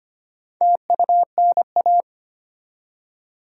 Morse code, heard as a keyed tone, about 25 wpm, 690 Hz. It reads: TUNA
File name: practice / 25wpm / Easy Words with Hints - 25wpm